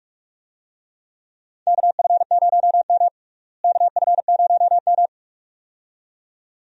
Morse code, heard as a keyed tone, about 44 wpm, 700 Hz. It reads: KF0M KF0M
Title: USA Call Signs - 4 Characters - Repeated 2x - 45wpm